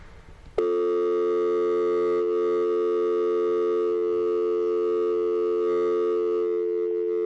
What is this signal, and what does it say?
Dial tone (North America), 350 + 440 Hz.